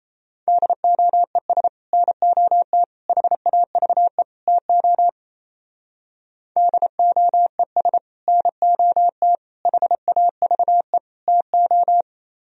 Morse code, heard as a keyed tone, about 37 wpm, 700 Hz. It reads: DOES NOT HAVE TO DOES NOT HAVE TO